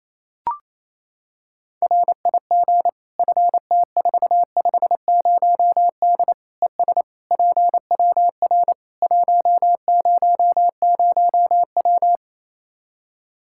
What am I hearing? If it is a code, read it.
Morse code, heard as a keyed tone, about 28 wpm, 700 Hz. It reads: RIG FT450D ES PWR 100W
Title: K1USN SST QSO Practice 28wpm